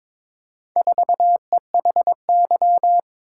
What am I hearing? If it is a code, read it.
Morse code, heard as a keyed tone, about 22 wpm, 700 Hz. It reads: 4EHY
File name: practice / 22wpm / Four Letter-Number Combinations - 22wpm